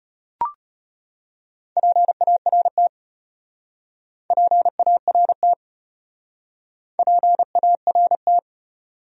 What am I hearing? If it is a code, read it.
Morse code, heard as a keyed tone, about 33 wpm, 700 Hz. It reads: PART PART PART